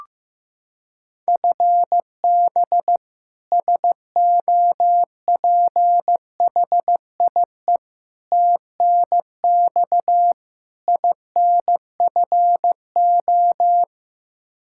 Morse code, heard as a keyed tone, about 15 wpm, 700 Hz. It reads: FB SOPHIE TNX INFO
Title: Practice QSO - Exchange 3 - More Info - 15wpm